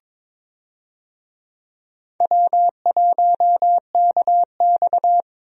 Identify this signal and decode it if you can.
Morse code, heard as a keyed tone, about 22 wpm, 700 Hz. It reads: W1KX
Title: USA Call Signs - 4 Characters - Repeated 2x - 22wpm